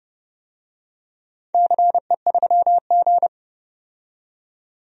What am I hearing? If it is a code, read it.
Morse code, heard as a keyed tone, about 32 wpm, 700 Hz. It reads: CE3G
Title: DXCC-100 Call Signs - 4 Characters - 30wpm